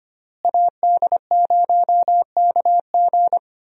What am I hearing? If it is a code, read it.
Morse code, heard as a keyed tone, about 25 wpm, 700 Hz. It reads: AD0KG
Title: USA Call Signs - 5 Characters - Repeated 3x - 25wpm